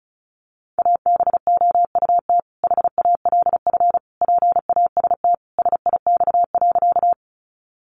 Morse code, heard as a keyed tone, about 34 wpm, 700 Hz. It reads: ABOUT HALF PAST SIX.
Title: Sentences from Top 600 Words 35wpm